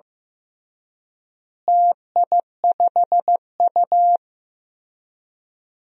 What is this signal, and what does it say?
Morse code, heard as a keyed tone, about 15 wpm, 700 Hz. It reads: TI5U
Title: DXCC-100 Call Signs - 4 Characters - Repeated 3x - 15wpm